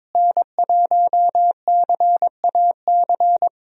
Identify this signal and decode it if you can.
Morse code, heard as a keyed tone, about 21 wpm, 700 Hz. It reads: N1CAC
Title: USA Call Signs - 5 Characters - Repeated 2x - 22wpm